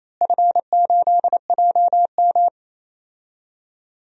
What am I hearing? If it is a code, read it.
Morse code, heard as a keyed tone, about 28 wpm, 700 Hz. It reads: F8JM
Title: DXCC-100 Call Signs - 4 Characters - Repeated 2x - 28wpm